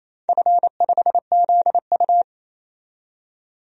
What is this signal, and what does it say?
Morse code, heard as a keyed tone, about 28 wpm, 700 Hz. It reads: F5ZU
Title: DXCC-100 Call Signs - 4 Characters - 28wpm